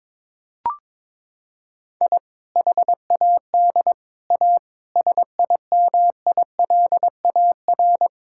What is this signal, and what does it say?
Morse code, heard as a keyed tone, about 22 wpm, 700 Hz. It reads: I HAD A SIMILAR
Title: Sentences from Top 1000 Words - Review Part 5 - 22wpm